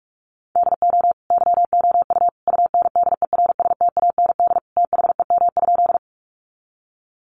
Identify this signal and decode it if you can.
Morse code, heard as a keyed tone, about 44 wpm, 700 Hz. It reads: DO YOU UNDERSTAND THEM?
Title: Sentences from Top 600 Words 45wpm